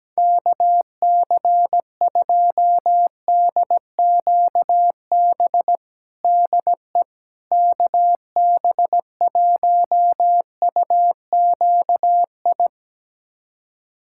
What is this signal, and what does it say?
Morse code, heard as a keyed tone, about 17 wpm, 700 Hz. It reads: KC2DQB DE KB1UQI
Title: USA Callsign DE Callsign 17wpm